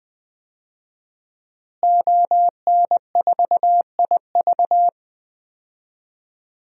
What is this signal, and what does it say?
Morse code, heard as a keyed tone, about 20 wpm, 700 Hz. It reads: ON4IV
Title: DXCC-100 Call Signs - 5 Characters - Repeated 3x - 20wpm